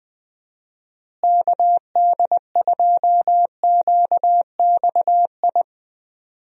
Morse code, heard as a keyed tone, about 20 wpm, 700 Hz. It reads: KD2QXI
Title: USA Call Signs - 6 Characters - Repeated 3x - 20wpm